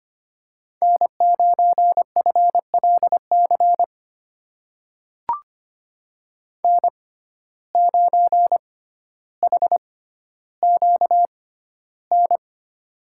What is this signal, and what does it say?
Morse code, heard as a keyed tone, about 25 wpm, spaced as for 13 wpm, 700 Hz. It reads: N9FLC N9HQN
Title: USA Call Signs - 5 Characters - 6x Character Spacing - 25wpm